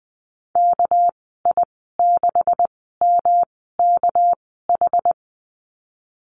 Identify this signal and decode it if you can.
Morse code, heard as a keyed tone, about 20 wpm, spaced as for 14 wpm, 700 Hz. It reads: KI6MKH
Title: 245 Lesson 39g - Letter z - Call Signs 20wpm